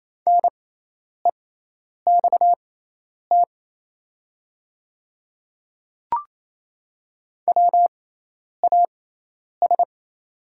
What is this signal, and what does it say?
Morse code, heard as a keyed tone, about 28 wpm, spaced as for 10 wpm, 700 Hz. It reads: NEXT WAS
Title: 6x Character Spacing - Top 300 Words - 28wpm